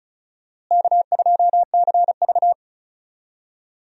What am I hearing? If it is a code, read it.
Morse code, heard as a keyed tone, about 36 wpm, 700 Hz. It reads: K2CV
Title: USA Call Signs - 4 Characters - Repeated 3x - 35wpm